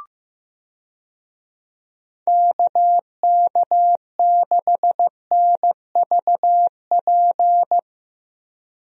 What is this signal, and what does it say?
Morse code, heard as a keyed tone, about 15 wpm, 700 Hz. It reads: KK6NVP
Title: USA Call Signs - 6 Characters - Repeated 3x - 15wpm